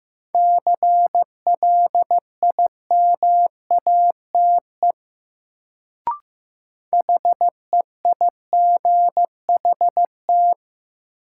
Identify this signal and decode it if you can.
Morse code, heard as a keyed tone, about 15 wpm, 700 Hz. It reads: CLIMATE HEIGHT
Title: Sentences from Top 1600 Words - New Word Drill Alt - 15wpm